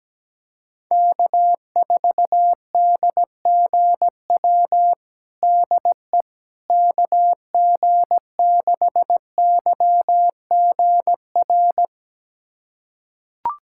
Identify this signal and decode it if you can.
Morse code, heard as a keyed tone, about 17 wpm, 700 Hz. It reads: K4DGW DE KG6YGR
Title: USA Callsign DE Callsign 17wpm